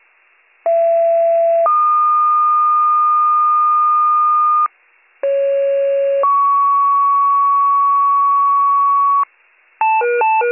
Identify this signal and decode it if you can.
Two-tone page, heard as a one-second tone, then a three-second tone, about 670 then 1220 Hz, about 550 then 1120 Hz. It is heard from 0.7 to 9.2 s.